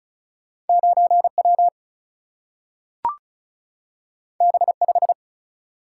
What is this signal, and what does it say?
Morse code, heard as a keyed tone, about 36 wpm, 700 Hz. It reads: 9 W B 5
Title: Single-Letter Single-Number 35wpm